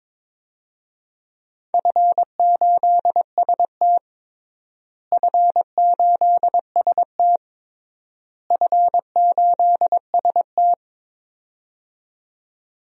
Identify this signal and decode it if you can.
Morse code, heard as a keyed tone, about 22 wpm, 700 Hz. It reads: F8ST F8ST F8ST
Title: DXCC-100 Call Signs - 4 Characters - Repeated 3x - 22wpm